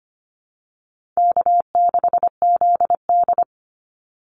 Morse code, heard as a keyed tone, about 25 wpm, 700 Hz. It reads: K6ZD
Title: Easy Call Signs - 25wpm